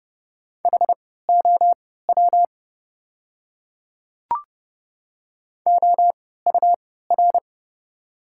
Morse code, heard as a keyed tone, about 30 wpm, spaced as for 17 wpm, 690 Hz. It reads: HOW OUR
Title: 3x Character Spacing - Top 200 Words - 30wpm